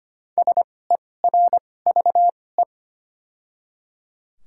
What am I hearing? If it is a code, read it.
Morse code, heard as a keyed tone, about 25 wpm, spaced as for 18 wpm, 700 Hz. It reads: SERVE
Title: 2x Character Spacing - Top 500 Words - 25wpm